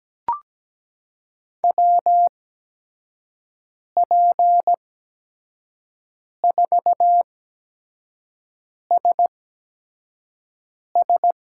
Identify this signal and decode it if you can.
Morse code, heard as a keyed tone, about 17 wpm, spaced as for 5 wpm, 700 Hz. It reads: WP4SS
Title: USA Call Signs - 5 Characters - 8x Character Spacing - 17wpm